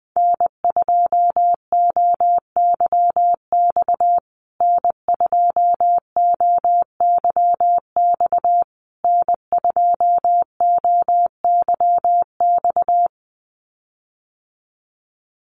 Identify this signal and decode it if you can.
Morse code, heard as a keyed tone, about 19 wpm, 700 Hz. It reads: N2OYX N2OYX N2OYX
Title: USA Call Signs Repeated 3x 20wpm